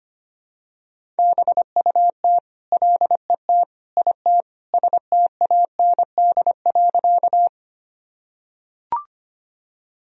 Morse code, heard as a keyed tone, about 25 wpm, 700 Hz. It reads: BUT LET IT STAND.
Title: Sentences from Top 1000 Words - Review Part 1 - 25wpm